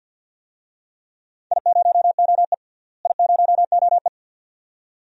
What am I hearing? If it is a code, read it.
Morse code, heard as a keyed tone, about 48 wpm, 700 Hz. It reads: I0OE I0OE
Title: DXCC-100 Call Signs - 4 Characters - Repeated 2x - 50wpm